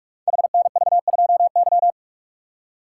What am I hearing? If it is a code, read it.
Morse code, heard as a keyed tone, about 44 wpm, 690 Hz. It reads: HNV2Y
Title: Code Groups - 5 Letter:Numbers - 45wpm